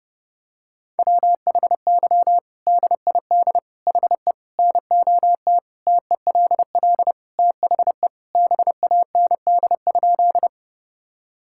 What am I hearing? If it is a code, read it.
Morse code, heard as a keyed tone, about 32 wpm, 700 Hz. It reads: WHY DID HE NOT TELL THE BAND?
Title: Sentences from Top 1600 Words 30wpm